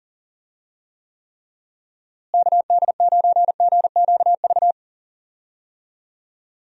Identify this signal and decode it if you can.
Morse code, heard as a keyed tone, about 40 wpm, 700 Hz. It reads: KD9GQV
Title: USA Call Signs - 6 Characters - Repeated 3x - 40wpm